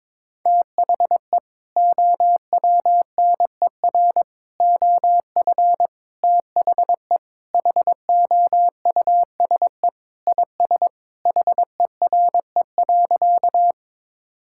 Morse code, heard as a keyed tone, about 21 wpm, 700 Hz. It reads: THE OWNER OF THE HOUSE IS HERE.
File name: practice / 22wpm / Sentences with 3-Word Phrases - 22wpm